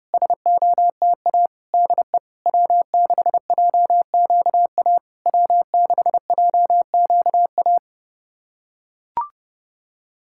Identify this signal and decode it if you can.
Morse code, heard as a keyed tone, about 30 wpm, 690 Hz. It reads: SOTA DE W6JQA W6JQA
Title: SOTA QSO Practice 30wpm